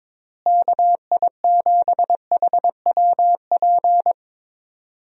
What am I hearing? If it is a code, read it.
Morse code, heard as a keyed tone, about 22 wpm, 700 Hz. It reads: KI7HWP
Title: USA Call Signs - 6 Characters - Repeated 3x - 22wpm